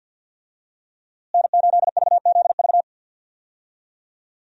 Morse code, heard as a keyed tone, about 52 wpm, 700 Hz. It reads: N8VZV
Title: USA Call Signs - 5 Characters - Repeated 3x - 50wpm